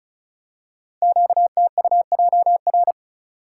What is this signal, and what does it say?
Morse code, heard as a keyed tone, about 36 wpm, 690 Hz. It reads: QTUJR
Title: Code Groups - 5 Letters - 35wpm